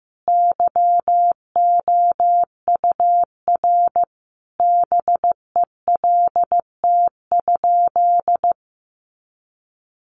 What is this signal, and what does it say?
Morse code, heard as a keyed tone, about 15 wpm, 700 Hz. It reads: YOUR BELT?
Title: Sentences from Top 1400 Words 15wpm